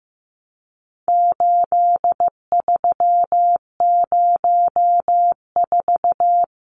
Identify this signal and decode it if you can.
Morse code, heard as a keyed tone, about 15 wpm, 700 Hz. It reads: 8304
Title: Numbers - 4-Digit 15wpm